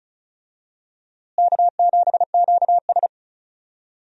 Morse code, heard as a keyed tone, about 36 wpm, 700 Hz. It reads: K7QS
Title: USA Call Signs - 4 Characters - Repeated 3x - 35wpm